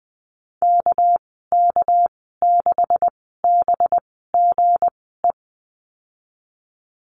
Morse code, heard as a keyed tone, about 19 wpm, spaced as for 14 wpm, 700 Hz. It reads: KK6BGE
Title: 200 Lesson 32g - Letter k - Call Signs 20wpm